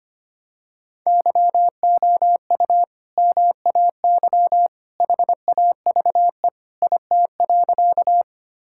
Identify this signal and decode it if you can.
Morse code, heard as a keyed tone, about 25 wpm, 690 Hz. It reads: YOU MAY HAVE IT.
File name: practice / 25wpm / Sentences with 2-Word Phrases - 25wpm